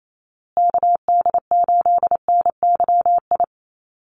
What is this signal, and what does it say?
Morse code, heard as a keyed tone, about 28 wpm, 700 Hz. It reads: KD8NYI
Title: USA Call Signs Repeated 3x 28wpm